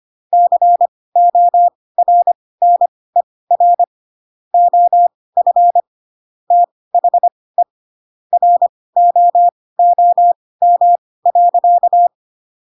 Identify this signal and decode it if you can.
Morse code, heard as a keyed tone, about 25 wpm, spaced as for 18 wpm, 700 Hz. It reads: CORNER OF THE ROOM.
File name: Pinocchio / Pinocchio ch7 12 25 ik1hge